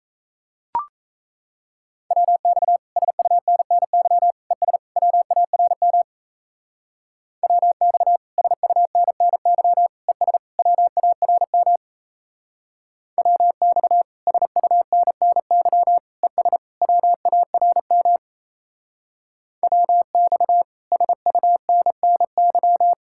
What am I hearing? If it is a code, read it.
Morse code, heard as a keyed tone, about 34 wpm, 700 Hz. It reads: WX SUNNY ES WARM WX SUNNY ES WARM WX SUNNY ES WARM WX SUNNY
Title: Speed-Racing Common QSO Phrases Encore - 28wpm